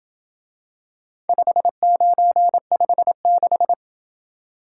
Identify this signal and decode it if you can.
Morse code, heard as a keyed tone, about 27 wpm, 700 Hz. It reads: 5956